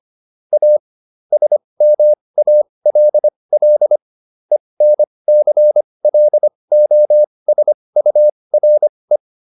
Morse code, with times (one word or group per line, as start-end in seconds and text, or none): 0.5-0.8 A
1.3-4.0 SMALL
4.5-9.2 ENCLOSURE